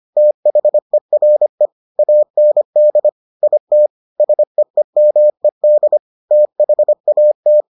Morse code, heard as a keyed tone, about 25 wpm, 600 Hz. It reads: THERE AND IT SEEMED THAT